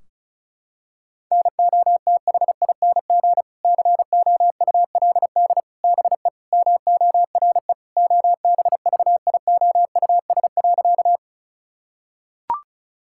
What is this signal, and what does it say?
Morse code, heard as a keyed tone, about 34 wpm, 700 Hz. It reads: NOTHING COULD BE MORE OBVIOUS.
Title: Sentences from Top 1600 Words 35wpm